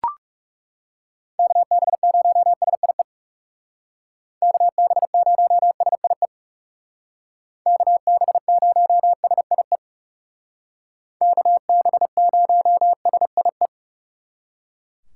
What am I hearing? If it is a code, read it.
Morse code, heard as a keyed tone, about 37 wpm, 700 Hz. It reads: KB0SIE KB0SIE KB0SIE KB0SIE
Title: Speed Racing - USA Call Signs 30wpm